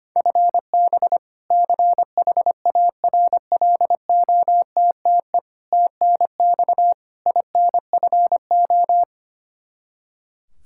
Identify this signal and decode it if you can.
Morse code, heard as a keyed tone, about 25 wpm, 700 Hz. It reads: FB CHARLOTTE TNX INFO
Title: Practice QSO - Exchange 3 - More Info - 25wpm